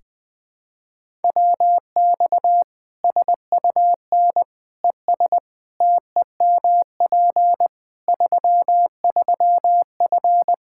Morse code, heard as a keyed tone, about 20 wpm, 700 Hz. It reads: WX SUN ES TEMP 33F
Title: K1USN SST QSO Practice 20wpm